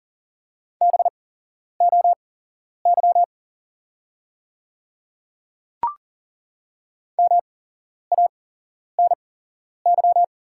Morse code, heard as a keyed tone, about 40 wpm, spaced as for 11 wpm, 690 Hz. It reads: BOY MANY
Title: 8x Character Spacing - Top 200 Words - 40wpm